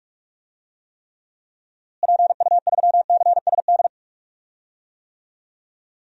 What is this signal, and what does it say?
Morse code, heard as a keyed tone, about 41 wpm, 700 Hz. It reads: PU3CSD
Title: DXCC-100 Call Signs - 6 Characters - Repeated 2x - 45wpm